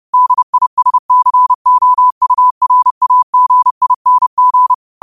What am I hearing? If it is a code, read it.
Morse code, heard as a keyed tone, about 30 wpm, 1000 Hz. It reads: DISCOURAGING
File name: qotdfiles / 30wpm